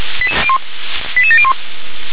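Five-tone selective call (ZVEI), heard as quick runs of five tones, tones of about 70 ms. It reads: 88701
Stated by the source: Received on SDR; ZVEI five-tone call 88852, channel 357.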